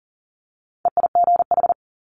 Morse code, heard as a keyed tone, about 42 wpm, 700 Hz. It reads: EIGH
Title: Multi-letter Phonemes Quick Fire 40wpm